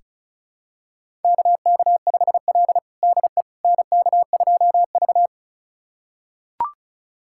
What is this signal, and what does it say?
Morse code, heard as a keyed tone, about 36 wpm, 700 Hz. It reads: KK5L DE NK2V
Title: USA Callsign DE Callsign - 4 Characters - 35wpm